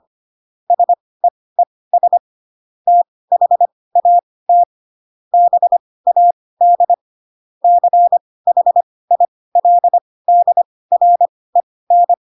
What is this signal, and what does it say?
Morse code, heard as a keyed tone, about 25 wpm, spaced as for 18 wpm, 700 Hz. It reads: SEES THAT BAD CHILDREN
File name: Pinocchio / Pinocchio ch4 12 25 ik1hge